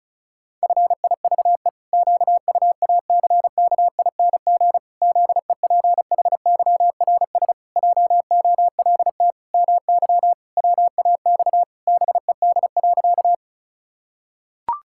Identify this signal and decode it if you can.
Morse code, heard as a keyed tone, about 36 wpm, 700 Hz. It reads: FIVE QUACKING ZEPHYRS JOLT MY WAX BED.
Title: Pangram Challenge v2 - 35wpm